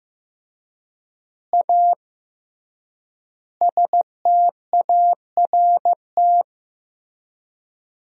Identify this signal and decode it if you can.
Morse code, heard as a keyed tone, about 15 wpm, 700 Hz. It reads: A START
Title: 3x Word-Spacing - Sentences from Top 300 Words - 15wpm